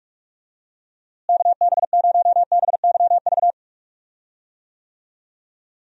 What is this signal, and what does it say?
Morse code, heard as a keyed tone, about 46 wpm, 700 Hz. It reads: KB0BYV
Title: USA Call Signs - 6 Characters - Repeated 3x - 45wpm